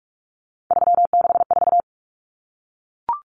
Morse code, heard as a keyed tone, about 44 wpm, 700 Hz. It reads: 364
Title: Common Cut Numbers - 3-Digit 45wpm